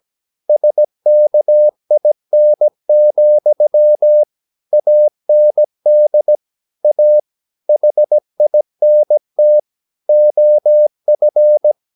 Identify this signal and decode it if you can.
Morse code, heard as a keyed tone, about 17 wpm, 600 Hz. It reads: SKIN, AND A HINT OF
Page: Romanen Never af Ken Follet på engelsk.